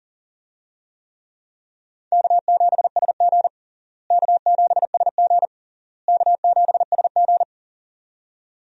Morse code, heard as a keyed tone, about 40 wpm, 700 Hz. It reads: K7SG K7SG K7SG
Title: USA Call Signs - 4 Characters - Repeated 3x - 40wpm